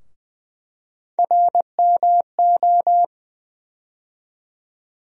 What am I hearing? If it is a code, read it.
Morse code, heard as a keyed tone, about 20 wpm, 700 Hz. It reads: RMO
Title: USA Call Sign Postfixes - 3 Letters 20wpm